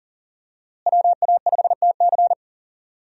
Morse code, heard as a keyed tone, about 40 wpm, 700 Hz. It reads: WA5TC
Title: USA Call Signs - 5 Characters - Repeated 3x - 40wpm